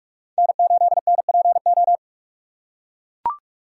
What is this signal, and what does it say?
Morse code, heard as a keyed tone, about 44 wpm, 700 Hz. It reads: N8NPO